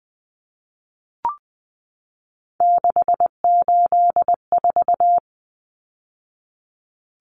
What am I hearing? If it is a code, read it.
Morse code, heard as a keyed tone, about 20 wpm, 700 Hz. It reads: 684
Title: Numbers - 3-Digit 20wpm